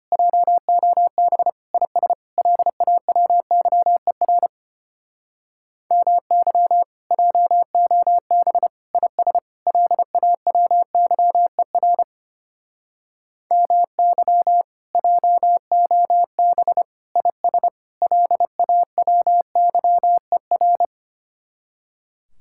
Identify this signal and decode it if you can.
Morse code, heard as a keyed tone, about 30 wpm, 700 Hz. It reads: JOB IS LAWYER MY JOB IS LAWYER MY JOB IS LAWYER
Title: Speed-Racing Common QSO Phrases Encore - 25wpm